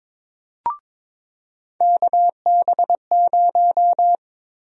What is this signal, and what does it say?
Morse code, heard as a keyed tone, about 21 wpm, 700 Hz. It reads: KB0
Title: USA Call Sign Prefixes - 22wpm